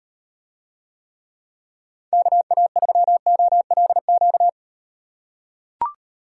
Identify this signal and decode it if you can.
Morse code, heard as a keyed tone, about 38 wpm, 700 Hz. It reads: KA3OLQ